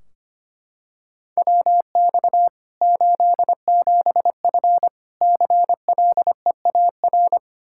Morse code, heard as a keyed tone, about 25 wpm, 700 Hz. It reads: WX 87F CLEAR
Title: Practice QSO - Exchange 2 - Info - 25wpm